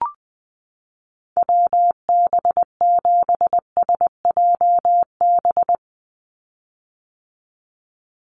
Morse code, heard as a keyed tone, about 20 wpm, 700 Hz. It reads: WB7SJB
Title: USA Call Signs 20wpm